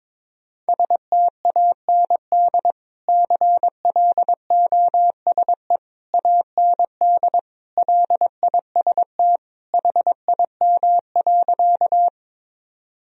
Morse code, heard as a keyed tone, about 22 wpm, 700 Hz. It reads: STAND CLOSE AND LIST HIM.
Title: Sentences from Top 1000 Words - Review Part 2 - 22wpm